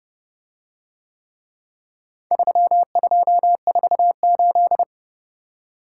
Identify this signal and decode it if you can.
Morse code, heard as a keyed tone, about 29 wpm, 700 Hz. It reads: 3248